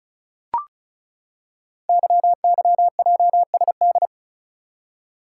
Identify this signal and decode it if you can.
Morse code, heard as a keyed tone, about 36 wpm, 700 Hz. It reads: YYJSD
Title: Code Groups - 5 Letters - 35wpm